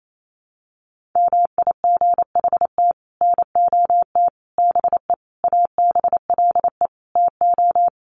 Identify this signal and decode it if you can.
Morse code, heard as a keyed tone, about 28 wpm, 700 Hz. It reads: MIGHT NOT BE ABLE TO
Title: Sets of 5 Words Encore2 28wpm